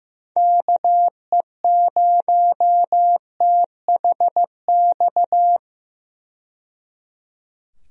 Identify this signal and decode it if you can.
Morse code, heard as a keyed tone, about 15 wpm, 700 Hz. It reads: KE0THX
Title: USA Call Signs - 6 Characters - 15wpm